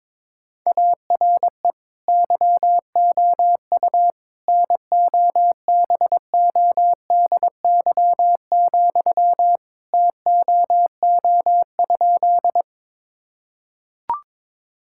Morse code, heard as a keyed tone, about 22 wpm, 700 Hz. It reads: ARE YOU NOBODY, TOO?
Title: Sentences from Top 1600 Words 22wpm